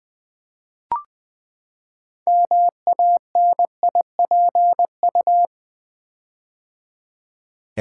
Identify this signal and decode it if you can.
Morse code, heard as a keyed tone, about 20 wpm, 700 Hz. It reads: MANIPU
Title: Build A Word - 6 Letters At A Time - 12-Letter Words - 20wpm